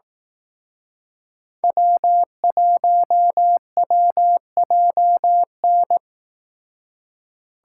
Morse code, heard as a keyed tone, about 18 wpm, 700 Hz. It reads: W1WJN
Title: Speed Racing - USA 1x3 Call Signs 15wpm